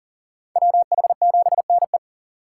Morse code, heard as a keyed tone, about 40 wpm, 700 Hz. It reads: WH7NE
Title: USA Call Signs - 5 Characters - 4x Character Spacing - 40wpm